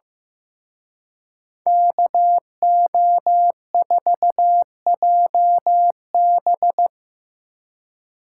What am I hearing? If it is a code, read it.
Morse code, heard as a keyed tone, about 17 wpm, 700 Hz. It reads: KO4JB
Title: Speed Racing - USA 2x2 Call Signs 15wpm